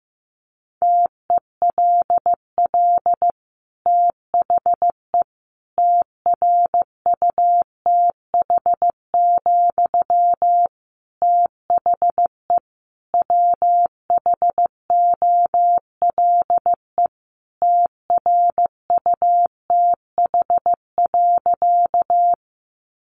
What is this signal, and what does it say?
Morse code, heard as a keyed tone, about 15 wpm, 700 Hz. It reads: TELL THE TRUTH, THE WHOLE TRUTH.
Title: Sentences from Top 800 Words 15wpm